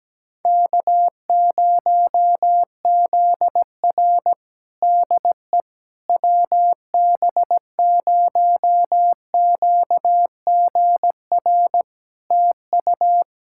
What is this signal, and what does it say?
Morse code, heard as a keyed tone, about 17 wpm, 700 Hz. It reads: K0ZR DE WB0QGR TU
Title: Practice QSO - Exchange 4 - Ending - 17wpm